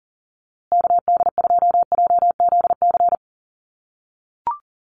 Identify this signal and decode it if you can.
Morse code, heard as a keyed tone, about 40 wpm, 700 Hz. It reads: KD2JZC
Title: USA Call Signs Repeated 2x 40wpm